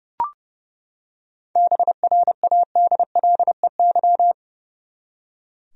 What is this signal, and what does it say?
Morse code, heard as a keyed tone, about 30 wpm, 700 Hz. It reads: BRADLEY
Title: USA Common Names - 30wpm